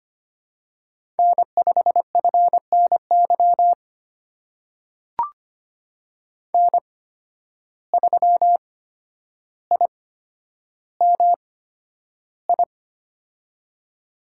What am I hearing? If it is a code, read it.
Morse code, heard as a keyed tone, about 25 wpm, spaced as for 11 wpm, 700 Hz. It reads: N5FNY N3IMI
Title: USA Call Signs - 5 Characters - 8x Character Spacing - 25wpm